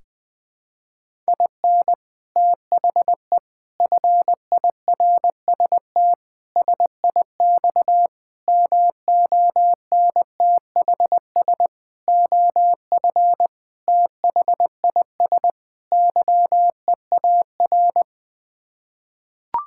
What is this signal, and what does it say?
Morse code, heard as a keyed tone, about 21 wpm, 700 Hz. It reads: IN THE FIRST SIX MONTHS OF THIS YEAR
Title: Sets of 8 Words - 20wpm